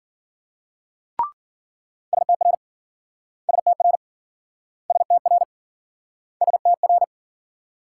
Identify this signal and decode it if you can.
Morse code, heard as a keyed tone, about 56 wpm, 700 Hz. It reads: STR STR STR STR